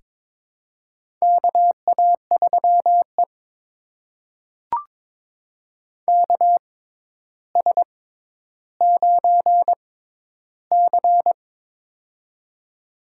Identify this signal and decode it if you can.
Morse code, heard as a keyed tone, about 22 wpm, spaced as for 11 wpm, 700 Hz. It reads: KA3E KS9C
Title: USA Call Signs - 4 Characters - 6x Character Spacing - 22wpm